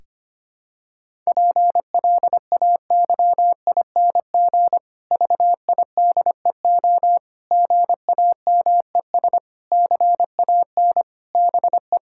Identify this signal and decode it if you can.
Morse code, heard as a keyed tone, about 25 wpm, 700 Hz. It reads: PLAYING VIDEO GAMES CAN BE
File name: practice / 25wpm / Difficult Sentences with Hints - 25wpm